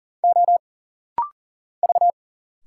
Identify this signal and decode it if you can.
Morse code, heard as a keyed tone, about 40 wpm, 700 Hz. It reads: O V
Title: Single Letters - Rapid-Fire 40wpm